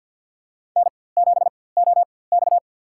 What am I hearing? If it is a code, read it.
Morse code, heard as a keyed tone, about 50 wpm, spaced as for 23 wpm, 700 Hz. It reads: N7OX
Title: USA Call Signs - 4x Character Spacing - 50wpm